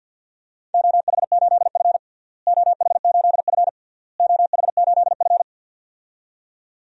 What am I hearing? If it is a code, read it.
Morse code, heard as a keyed tone, about 52 wpm, 700 Hz. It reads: OH8F OH8F OH8F
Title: DXCC-100 Call Signs - 4 Characters - Repeated 3x - 50wpm